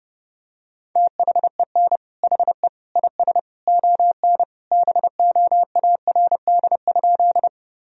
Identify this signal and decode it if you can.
Morse code, heard as a keyed tone, about 30 wpm, 700 Hz. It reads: THEN HE IS ON BOARD?
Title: Sentences from Top 1000 Words - Review Part 4 - 30wpm